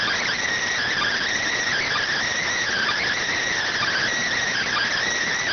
Emergency Alert System (SAME) tones, heard as harsh buzzing bursts. It is heard from 0.9 to 4.9 s.